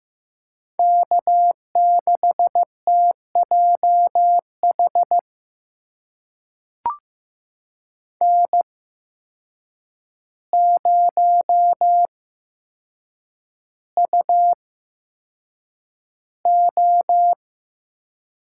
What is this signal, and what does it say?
Morse code, heard as a keyed tone, about 15 wpm, spaced as for 7 wpm, 700 Hz. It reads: K6TJH N0UO
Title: USA Call Signs - 5 Characters - 8x Character Spacing - 15wpm